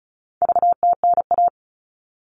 Morse code, heard as a keyed tone, about 34 wpm, 700 Hz. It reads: VTNA
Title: All Cut Numbers - 4-Digit 35wpm